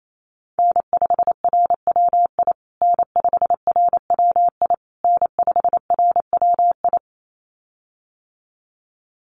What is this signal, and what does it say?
Morse code, heard as a keyed tone, about 28 wpm, 700 Hz. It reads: N5RWI N5RWI N5RWI
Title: USA Call Signs Repeated 3x 28wpm